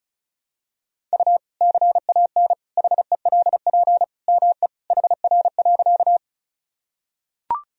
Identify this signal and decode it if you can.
Morse code, heard as a keyed tone, about 34 wpm, 700 Hz. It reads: U CAN HELP ME HR.
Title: Sentences from Top 200 Words and Abbreviations 35wpm